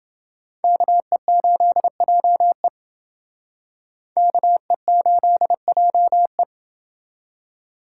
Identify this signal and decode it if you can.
Morse code, heard as a keyed tone, about 28 wpm, 700 Hz. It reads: KE8JE KE8JE